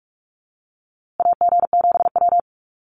Morse code, heard as a keyed tone, about 44 wpm, 700 Hz. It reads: AG7W
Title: USA Call Signs Repeated 3x 45wpm